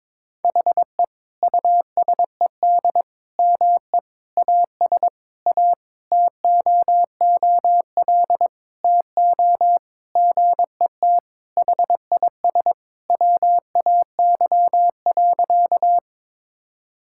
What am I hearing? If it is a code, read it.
Morse code, heard as a keyed tone, about 21 wpm, 700 Hz. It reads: HE USED ME AS A TOOL TO GET HIS WAY.
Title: Sentences from Top 1000 Words - Review Part 6 - 22wpm